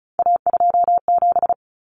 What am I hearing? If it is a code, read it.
Morse code, heard as a keyed tone, about 34 wpm, 700 Hz. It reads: A27
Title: Common Cut Numbers - 3-Digit 35wpm